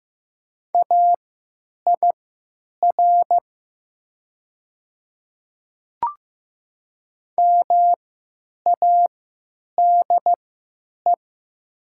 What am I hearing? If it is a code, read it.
Morse code, heard as a keyed tone, about 15 wpm, spaced as for 9 wpm, 700 Hz. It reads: AIR MADE
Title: 3x Character Spacing - Top 300 Words - 15wpm